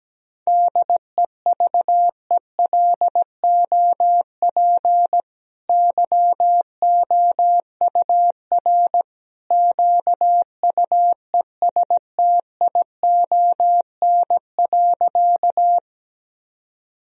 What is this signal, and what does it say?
Morse code, heard as a keyed tone, about 17 wpm, 700 Hz. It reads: DEVELOP YOUR QUESTION.
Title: Sentences from Top 1000 Words - Review Part 2 - 17wpm